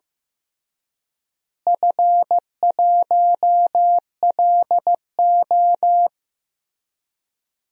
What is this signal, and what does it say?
Morse code, heard as a keyed tone, about 15 wpm, 700 Hz. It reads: F1LO
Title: DXCC-100 Call Signs - 4 Characters - Repeated 2x - 15wpm